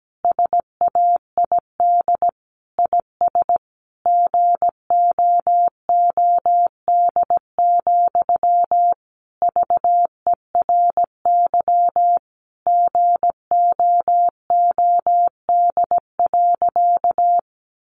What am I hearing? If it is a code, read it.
Morse code, heard as a keyed tone, about 17 wpm, 700 Hz. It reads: SAID IS GOOD, VERY GOOD.
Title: Sentences from Top 200 Words 17wpm